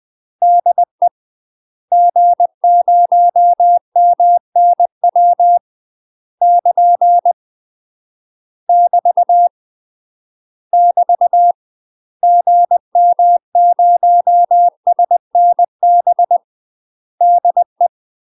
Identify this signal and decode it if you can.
Morse code, heard as a keyed tone, about 20 wpm, 700 Hz. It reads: DE G0MNW <KN> <BT> <BT> GM0SNB DE
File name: UK MASTER QSOs TRAINING / 20WPM / UK MASTER QSOs 0023